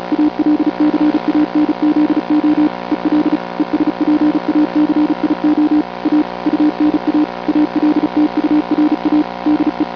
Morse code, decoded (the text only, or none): ALCANZO EL ESPACIO A UNA ALTURA DE